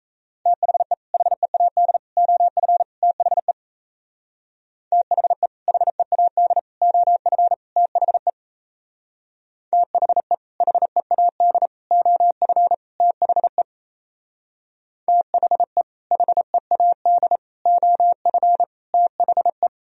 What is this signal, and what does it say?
Morse code, heard as a keyed tone, about 34 wpm, 700 Hz. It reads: THE HEAD OF THE THE HEAD OF THE THE HEAD OF THE THE HEAD OF THE
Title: Speed-Racing - Sets of 4 Words 28wpm